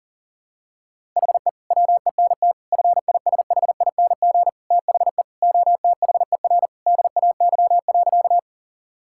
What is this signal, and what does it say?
Morse code, heard as a keyed tone, about 37 wpm, 700 Hz. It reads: HE WENT FISHING THE OTHER DAY.
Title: Sentences from Top 1000 Words - Review Part 5 - 40wpm